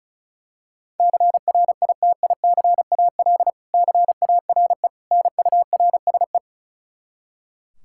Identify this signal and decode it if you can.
Morse code, heard as a keyed tone, about 34 wpm, 700 Hz. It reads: CRITICAL CARE NURSE
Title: Practice QSO - Exchange 3 - More Info - 35wpm